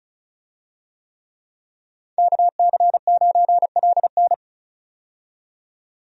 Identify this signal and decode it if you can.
Morse code, heard as a keyed tone, about 36 wpm, 700 Hz. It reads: KC9LN
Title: USA Call Signs - 5 Characters - 35wpm